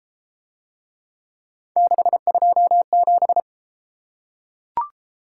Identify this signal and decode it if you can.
Morse code, heard as a keyed tone, about 31 wpm, 700 Hz. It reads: 627